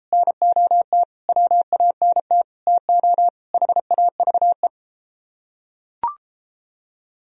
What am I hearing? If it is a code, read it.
Morse code, heard as a keyed tone, about 32 wpm, 700 Hz. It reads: NOT WANT TO HAVE